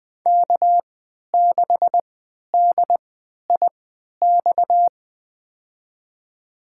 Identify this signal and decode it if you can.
Morse code, heard as a keyed tone, about 20 wpm, spaced as for 11 wpm, 700 Hz. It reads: K6DIX
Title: USA Call Signs - 3x Character Spacing - 20wpm